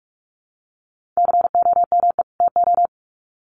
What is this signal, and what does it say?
Morse code, heard as a keyed tone, about 44 wpm, 700 Hz. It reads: COME TO
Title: Sets of 2 Words 45wpm